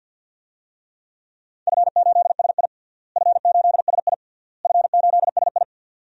Morse code, heard as a keyed tone, about 53 wpm, 700 Hz. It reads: F8SI F8SI F8SI
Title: DXCC-100 Call Signs - 4 Characters - Repeated 3x - 50wpm